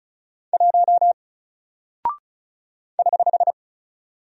Morse code, heard as a keyed tone, about 36 wpm, 700 Hz. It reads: 1 <HH>
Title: Single Character (Letter:Number:Punctuation:Prosign) Alt 35wpm